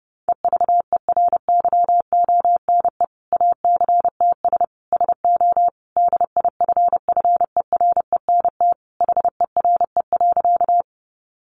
Morse code, heard as a keyed tone, about 30 wpm, 700 Hz. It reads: EVERYONE ACTS SO DIFFERENT HERE.
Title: Easy Phrases 30wpm